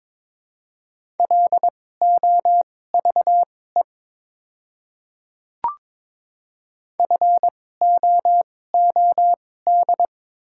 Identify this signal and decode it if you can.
Morse code, heard as a keyed tone, about 21 wpm, spaced as for 16 wpm, 700 Hz. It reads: LOVE FOOD
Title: 2x Character Spacing - Top 500 Words - 22wpm